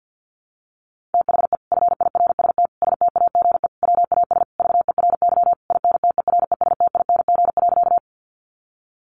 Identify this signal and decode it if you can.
Morse code, heard as a keyed tone, about 50 wpm, 700 Hz. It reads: THE FIRST STAGE WAS VERY INTERESTING.
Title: Sentences from Top 1200 Words 50wpm